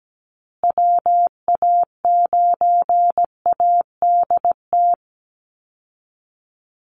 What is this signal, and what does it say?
Morse code, heard as a keyed tone, about 17 wpm, 700 Hz. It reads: WA9ADT
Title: Call Signs Repeated 3x - 17wpm